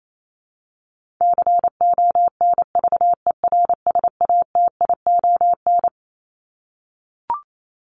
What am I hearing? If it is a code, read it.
Morse code, heard as a keyed tone, about 28 wpm, 700 Hz. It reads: CONVERSATION
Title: Top 1751-2000 Words 28wpm